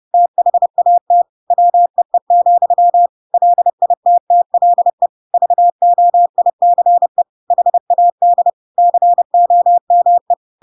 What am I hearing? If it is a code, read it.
Morse code, heard as a keyed tone, about 30 wpm, 700 Hz. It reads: THAT WEE, LITTLE VOICE HAD COME
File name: Pinocchio / Pinocchio ch1 30 30 ik1hge